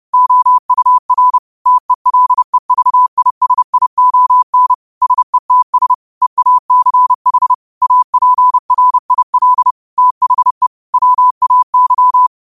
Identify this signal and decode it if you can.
Morse code, heard as a keyed tone, about 30 wpm, 1000 Hz. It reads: OUR TELEVISION SETS EACH APRIL THE WAY
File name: qotdfiles / 30wpm